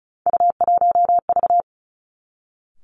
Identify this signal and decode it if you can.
Morse code, heard as a keyed tone, about 36 wpm, 700 Hz. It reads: U1V
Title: Three Challenging Letter-Number Combinations 35wpm